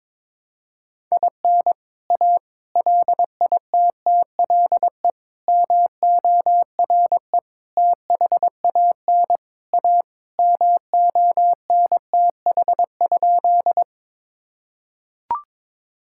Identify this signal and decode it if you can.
Morse code, heard as a keyed tone, about 22 wpm, 700 Hz. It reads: IN A LITTLE MORE THAN A MONTH?
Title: Sentences from Top 1000 Words - Review Part 5 - 22wpm